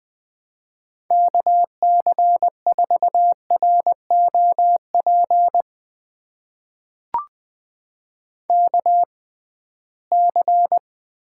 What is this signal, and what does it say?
Morse code, heard as a keyed tone, about 20 wpm, spaced as for 15 wpm, 700 Hz. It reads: KC4ROP KC
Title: USA Call Signs - 6 Characters - 6x Character Spacing - 20wpm